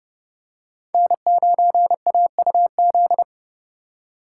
Morse code, heard as a keyed tone, about 30 wpm, 700 Hz. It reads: N9AUZ
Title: USA Call Signs - 4x Character Spacing - 30wpm